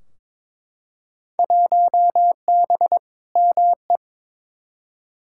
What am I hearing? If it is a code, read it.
Morse code, heard as a keyed tone, about 22 wpm, 700 Hz. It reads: 1B ME
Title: Field Day Exchanges - 22wpm